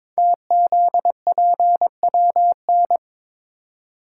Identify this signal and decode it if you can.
Morse code, heard as a keyed tone, about 23 wpm, 700 Hz. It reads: TZPWN
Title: Code Groups - 5 Letter:Numbers - 22wpm